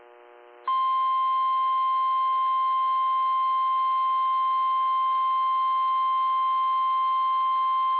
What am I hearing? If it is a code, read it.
Emergency Alert System (SAME) tones (the 1050 Hz attention tone). It is heard from 1.0 to 7.6 s.
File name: eas / KIMB / KIMB